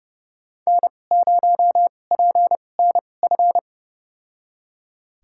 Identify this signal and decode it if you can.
Morse code, heard as a keyed tone, about 30 wpm, spaced as for 22 wpm, 700 Hz. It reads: N0PNF
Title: USA Call Signs - 2x Character Spacing - 30wpm